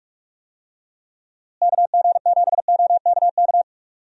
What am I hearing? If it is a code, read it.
Morse code, heard as a keyed tone, about 46 wpm, 700 Hz. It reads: KG7OKK
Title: USA Call Signs - 6 Characters - Repeated 2x - 45wpm